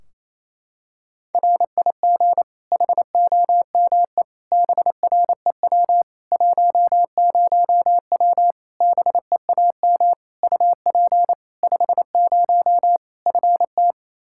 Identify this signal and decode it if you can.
Morse code, heard as a keyed tone, about 28 wpm, 700 Hz. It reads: RIG HOME BREW 10W BEAM UP 50 FT
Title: Practice QSO - Exchange 2 - Info - 28wpm